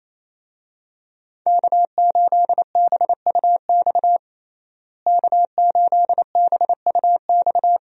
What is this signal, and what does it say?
Morse code, heard as a keyed tone, about 28 wpm, 700 Hz. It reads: K8BUX K8BUX
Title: USA Call Signs - 5 Characters - Repeated 3x - 28wpm